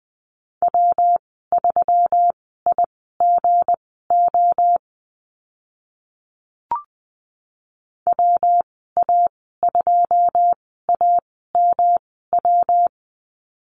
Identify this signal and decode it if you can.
Morse code, heard as a keyed tone, about 20 wpm, spaced as for 14 wpm, 700 Hz. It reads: W3IGO WA2AMW
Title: 161 Lesson 26g - Letter g - Call Signs 20wpm